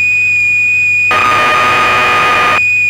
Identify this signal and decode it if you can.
Cassette data (Kansas City type).
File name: Acorn User/1984-12 Electron Programs